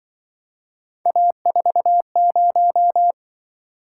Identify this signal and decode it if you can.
Morse code, heard as a keyed tone, about 25 wpm, 700 Hz. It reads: A40
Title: Speed-Racing - Three Letter-Number Combinations - 20wpm